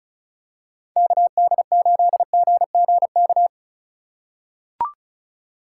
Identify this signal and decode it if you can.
Morse code, heard as a keyed tone, about 34 wpm, 700 Hz. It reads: KD8GGK
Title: USA Call Signs - 6 Characters - 4x Character Spacing - 35wpm